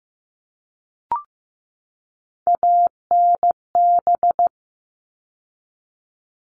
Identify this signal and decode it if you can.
Morse code, heard as a keyed tone, about 15 wpm, 700 Hz. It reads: ANB
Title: All Cut Numbers - 3-Digit 15wpm